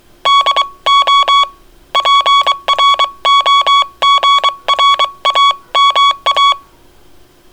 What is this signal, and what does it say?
Morse code, heard as a keyed tone, about 23 wpm, 1120 Hz. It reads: DO PROGRAMA